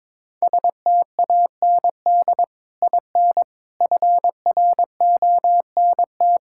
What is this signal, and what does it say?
Morse code, heard as a keyed tone, about 22 wpm, 700 Hz. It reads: STAND IN FRONT
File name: practice / 22wpm / Sentences with 3-Word Phrases - 22wpm